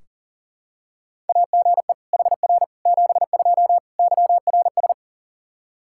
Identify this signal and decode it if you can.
Morse code, heard as a keyed tone, about 40 wpm, 700 Hz. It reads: AGE HR 72 YRS
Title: Practice QSO - Exchange 3 - More Info - 40wpm